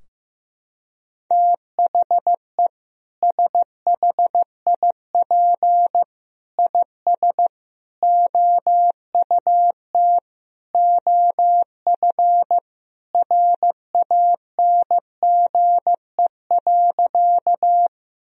Morse code, heard as a keyed tone, about 15 wpm, 700 Hz. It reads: THE SHIP IS OUT OF RANGE.
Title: Sentences from Top 1000 Words - Review Part 4 - 15wpm